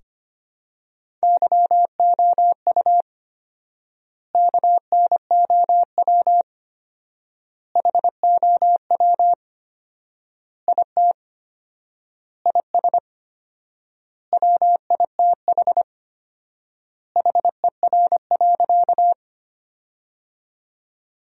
Morse code, heard as a keyed tone, about 25 wpm, 700 Hz. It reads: YOU KNOW HOW IT IS WITH HER.
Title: 4x Word-Spacing - Sentences from Top 100 Words - 25wpm